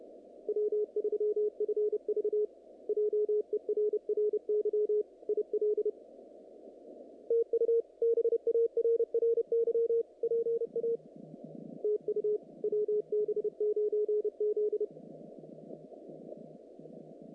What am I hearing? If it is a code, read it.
Morse code, heard as a keyed tone, about 28 wpm, 440 Hz. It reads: W3FV JERRY IL TU BARRY PA TU WB9Z
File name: contests / naqp-cw-k5tr-op-k5pi- -2200